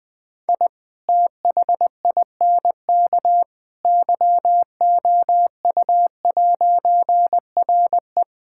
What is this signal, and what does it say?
Morse code, heard as a keyed tone, about 20 wpm, 700 Hz. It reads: I THINK YOU'RE
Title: All-Age Appropriate Jokes - 20wpm